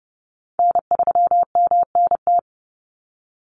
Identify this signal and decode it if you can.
Morse code, heard as a keyed tone, about 30 wpm, 700 Hz. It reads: N3MNT
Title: Call Signs Repeated 2x - 30wpm